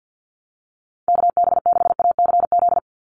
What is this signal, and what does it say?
Morse code, heard as a keyed tone, about 49 wpm, 700 Hz. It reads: KB6ACZ
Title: USA Call Signs 50wpm